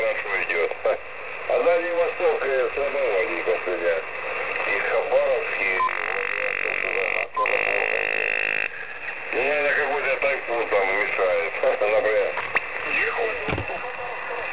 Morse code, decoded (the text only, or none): none